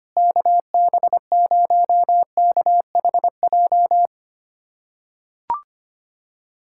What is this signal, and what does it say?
Morse code, heard as a keyed tone, about 25 wpm, 700 Hz. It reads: KB0KHJ
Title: USA Callsign DE Callsign - 6 Characters - 25wpm